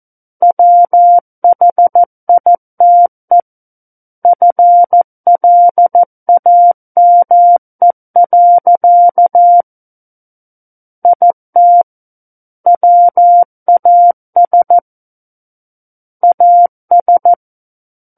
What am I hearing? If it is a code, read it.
Morse code, heard as a keyed tone, about 14 wpm, 700 Hz. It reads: WHITE FLAME. IT WAS AS
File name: Book One / 05-Chapter 5 - 14 WPM